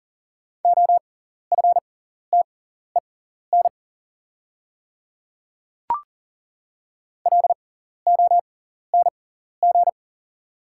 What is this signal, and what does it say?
Morse code, heard as a keyed tone, about 38 wpm, spaced as for 14 wpm, 700 Hz. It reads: OFTEN LONG
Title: 6x Character Spacing - Top 500 Words - 40wpm